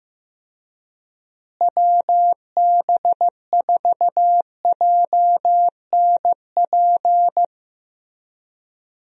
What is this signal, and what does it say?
Morse code, heard as a keyed tone, about 15 wpm, 700 Hz. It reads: WB4JNP
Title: USA Call Signs - 6 Characters - Repeated 3x - 15wpm